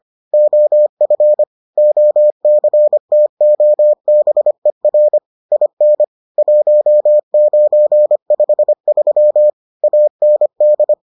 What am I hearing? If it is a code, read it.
Morse code, heard as a keyed tone, about 25 wpm, 600 Hz. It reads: OF OCTOBER IN 1953 AND